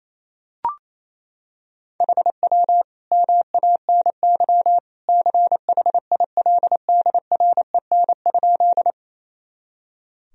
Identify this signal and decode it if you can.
Morse code, heard as a keyed tone, about 28 wpm, 700 Hz. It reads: HW MANY CHILDREN?
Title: Sentences from Top 300 Words and Abbreviations 28wpm